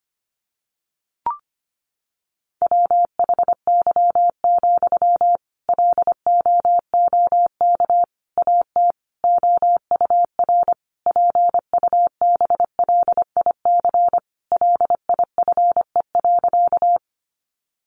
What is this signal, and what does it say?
Morse code, heard as a keyed tone, about 25 wpm, 700 Hz. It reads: WHY, LOOK AT OUR PUBLIC LIFE.
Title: Sentences from Top 600 Words 25wpm